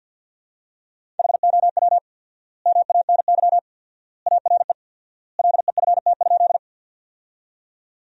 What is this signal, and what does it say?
Morse code, heard as a keyed tone, about 52 wpm, 700 Hz. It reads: HOW MANY ARE LEFT?
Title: 4x Word-Spacing - Sentences from Top 300 Words - 50wpm